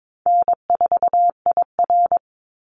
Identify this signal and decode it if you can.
Morse code, heard as a keyed tone, about 22 wpm, 700 Hz. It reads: N4IR
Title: Easy Call Signs - 22wpm